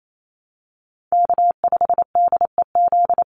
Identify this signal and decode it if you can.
Morse code, heard as a keyed tone, about 28 wpm, 700 Hz. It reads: K5DEZ
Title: NA SOTA Chaser Callsigns 28wpm